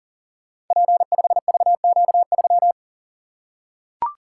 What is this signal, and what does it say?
Morse code, heard as a keyed tone, about 42 wpm, 700 Hz. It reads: P5VQ3
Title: Code Groups - 5 Letter:Numbers - 40wpm